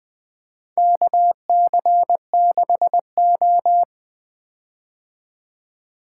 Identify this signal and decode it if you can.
Morse code, heard as a keyed tone, about 20 wpm, 700 Hz. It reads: KC6O
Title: Four Letter-Number Combinations - 20wpm